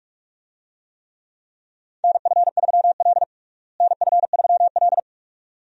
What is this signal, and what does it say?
Morse code, heard as a keyed tone, about 46 wpm, 700 Hz. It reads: NF3L NF3L
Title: USA Call Signs - 4 Characters - Repeated 2x - 45wpm